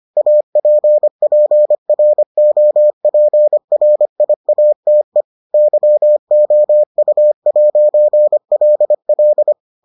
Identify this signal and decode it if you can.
Morse code, heard as a keyed tone, about 25 wpm, 600 Hz. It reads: APPROPRIATE YOU'LL